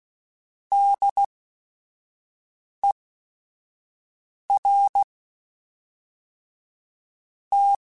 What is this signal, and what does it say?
Morse code, heard as a keyed tone, about 16 wpm, spaced as for 4 wpm, 800 Hz. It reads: DERT